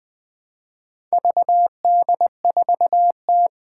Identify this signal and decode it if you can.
Morse code, heard as a keyed tone, about 20 wpm, 700 Hz. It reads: VD4T
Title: Four Challenging Letter-Number Combinations 20wpm